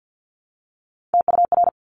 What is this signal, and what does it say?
Morse code, heard as a keyed tone, about 50 wpm, 700 Hz. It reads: TUR
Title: Top Three Letter Combinations in Words 50wpm